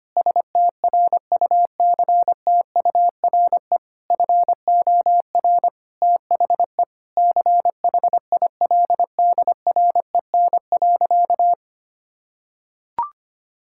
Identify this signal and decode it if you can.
Morse code, heard as a keyed tone, about 25 wpm, 700 Hz. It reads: STRUCTURE FOR THE CHILDREN.
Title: Sentences from Top 1000 Words - Review Part 6 - 25wpm